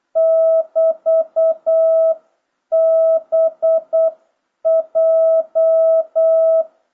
Morse code, heard as a keyed tone, about 8 wpm, 660 Hz. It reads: <BT>BJ